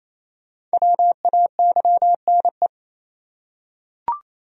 Morse code, heard as a keyed tone, about 27 wpm, 700 Hz. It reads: WAYNE
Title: USA Common Names - Max 5 Letters - 28wpm